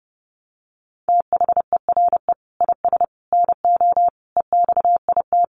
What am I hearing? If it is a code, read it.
Morse code, heard as a keyed tone, about 30 wpm, 700 Hz. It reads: THERE IS NO EXIT
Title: Sentences from Top 1100 Words 30wpm